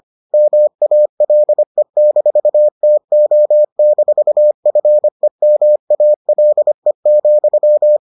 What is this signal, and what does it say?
Morse code, heard as a keyed tone, about 25 wpm, 600 Hz. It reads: MALE-TO-FEMALE,